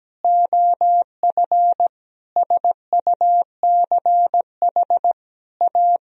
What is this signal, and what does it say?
Morse code, heard as a keyed tone, about 17 wpm, 700 Hz. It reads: OF SUCH A
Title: Sentences from Top 200 Words and Abbreviations 17wpm